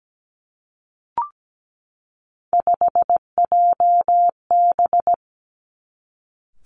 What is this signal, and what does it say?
Morse code, heard as a keyed tone, about 17 wpm, 700 Hz. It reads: 5JB
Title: Three Challenging Letter-Number Combinations 17wpm